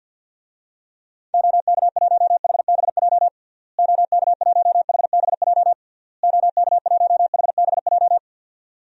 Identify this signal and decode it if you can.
Morse code, heard as a keyed tone, about 47 wpm, 700 Hz. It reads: OK1HBJ OK1HBJ OK1HBJ
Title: DXCC-100 Call Signs - 6 Characters - Repeated 3x - 50wpm